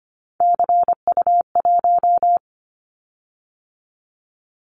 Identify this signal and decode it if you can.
Morse code, heard as a keyed tone, about 25 wpm, 700 Hz. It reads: CU1
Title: Callsign Prefixes for Top 50 DXCC - 25wpm